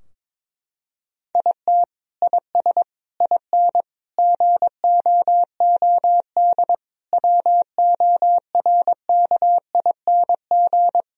Morse code, heard as a keyed tone, about 22 wpm, 700 Hz. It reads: IT IS IN GOOD WORKING
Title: Sentences from Top 1000 Words - Review Part 4 - 22wpm